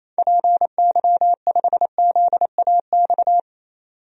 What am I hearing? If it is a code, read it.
Morse code, heard as a keyed tone, about 28 wpm, 700 Hz. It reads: PY5ZAX
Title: DXCC-100 Call Signs - 6 Characters - 28wpm